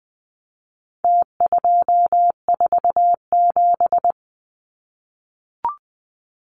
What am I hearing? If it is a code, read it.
Morse code, heard as a keyed tone, about 20 wpm, 700 Hz. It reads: T247
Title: Numbers - Contest Style - 4-Digit 20wpm